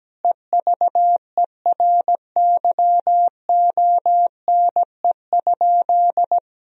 Morse code, heard as a keyed tone, about 17 wpm, 700 Hz. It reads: EVERYONE?
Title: Sentences from Top 1000 Words - Review Part 4 - 17wpm